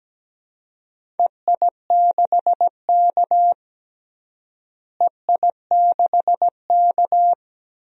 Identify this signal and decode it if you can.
Morse code, heard as a keyed tone, about 17 wpm, 700 Hz. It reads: EI6K EI6K
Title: DXCC-100 Call Signs - 4 Characters - Repeated 2x - 17wpm